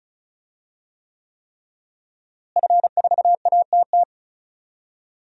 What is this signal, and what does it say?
Morse code, heard as a keyed tone, about 36 wpm, 700 Hz. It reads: F4ATT
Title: DXCC-100 Call Signs - 5 Characters - Repeated 3x - 35wpm